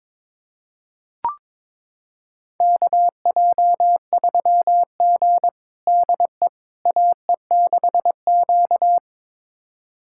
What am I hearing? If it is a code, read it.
Morse code, heard as a keyed tone, about 22 wpm, 700 Hz. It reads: KJ3G DE AE6Q
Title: USA Callsign DE Callsign - 4 Characters - 22wpm